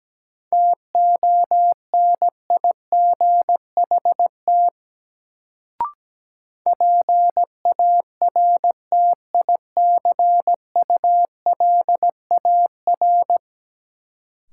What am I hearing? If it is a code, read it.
Morse code, heard as a keyed tone, about 17 wpm, 700 Hz. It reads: TONIGHT PARTICULAR
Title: Sentences from Top 600 Words - New Word Drill - 17wpm